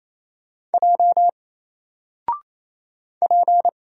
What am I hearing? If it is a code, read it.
Morse code, heard as a keyed tone, about 28 wpm, 700 Hz. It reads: J P
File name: Single-Letter-Number-28wpm